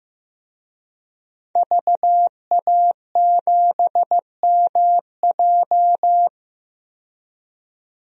Morse code, heard as a keyed tone, about 15 wpm, 700 Hz. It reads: VA7MJ
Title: DXCC-100 Call Signs - 5 Characters - 15wpm